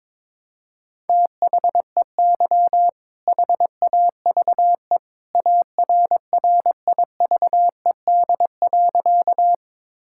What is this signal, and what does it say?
Morse code, heard as a keyed tone, about 22 wpm, 700 Hz. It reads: THEY HAVE ARRIVED.
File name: practice / 22wpm / Sentences with 2-Word Phrases - 22wpm